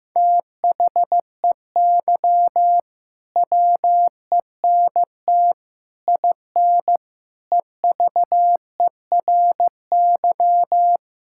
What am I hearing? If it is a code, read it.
Morse code, heard as a keyed tone, about 16 wpm, 700 Hz. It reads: THEY WENT IN EVERY
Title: Sentences from Top 1000 Words - Review Part 5 - 15wpm